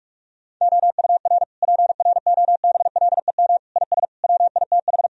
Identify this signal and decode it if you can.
Morse code, heard as a keyed tone, about 46 wpm, 700 Hz. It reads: OUR PROBLEM IS WITH
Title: Sentences from Top 1000 Words - Review Part 2 - 45wpm